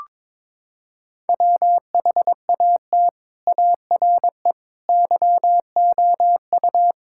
Morse code, heard as a keyed tone, about 22 wpm, 700 Hz. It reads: WHAT ARE YOU
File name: practice / 22wpm / Sentences with 8-Word Phrases - 22wpm